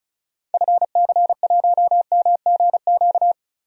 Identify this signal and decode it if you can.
Morse code, heard as a keyed tone, about 36 wpm, 700 Hz. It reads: FC1MGQ
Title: DXCC-100 Call Signs - 6 Characters - Repeated 2x - 35wpm